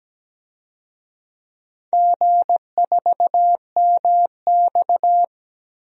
Morse code, heard as a keyed tone, about 17 wpm, 700 Hz. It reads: G4MX
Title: DXCC-100 Call Signs - 4 Characters - Repeated 3x - 17wpm